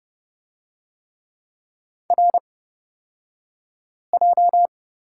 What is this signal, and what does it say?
Morse code, heard as a keyed tone, about 30 wpm, spaced as for 5 wpm, 700 Hz. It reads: RJ
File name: Lesson 38d - Letter q - Two Characters 30wpm